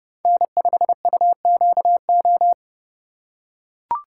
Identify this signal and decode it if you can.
Morse code, heard as a keyed tone, about 30 wpm, 700 Hz. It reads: N5UQO
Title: USA Call Signs - 5 Characters - 2x Character Spacing - 30wpm